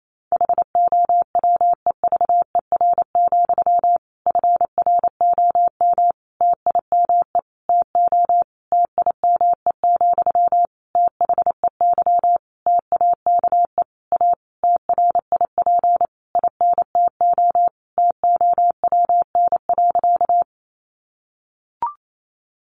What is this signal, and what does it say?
Morse code, heard as a keyed tone, about 28 wpm, 700 Hz. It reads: HOWEVER, FROM TIME TO TIME, THEY TAKE A TRIP INTO TOWN.
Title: Sentences from Top 1300 Words 28wpm